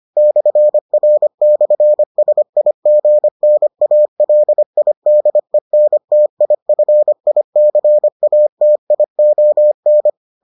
Morse code, heard as a keyed tone, about 25 wpm, 600 Hz. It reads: /R/SIGNALIDENTIFICATION